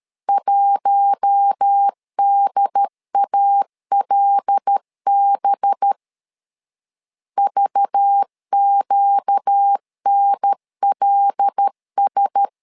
Morse code, heard as a keyed tone, about 13 wpm, 800 Hz. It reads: 1DALB VQNLS
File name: LEZIONI / 5 LBVC / 5 LBVC 12wpm 1955